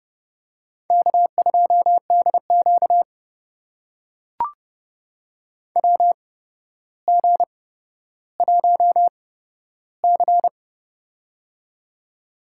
Morse code, heard as a keyed tone, about 30 wpm, spaced as for 13 wpm, 700 Hz. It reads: K2DQ WG1C
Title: USA Call Signs - 8x Character Spacing - 30wpm